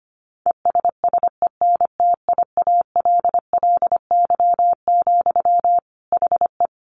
Morse code, heard as a keyed tone, about 25 wpm, 700 Hz. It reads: ESSENTIALLY, HE
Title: Sentences from Top 1300 Words 25wpm